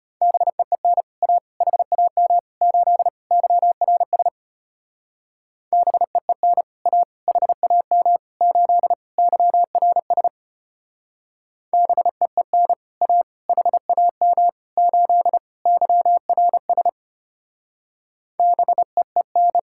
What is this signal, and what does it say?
Morse code, heard as a keyed tone, about 32 wpm, 700 Hz. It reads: BEEN A HAM 8 YRS BEEN A HAM 8 YRS BEEN A HAM 8 YRS BEEN